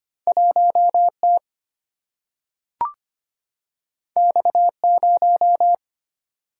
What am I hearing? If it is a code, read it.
Morse code, heard as a keyed tone, about 25 wpm, 700 Hz. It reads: 1T X0
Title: Single-Letter Single-Number 25wpm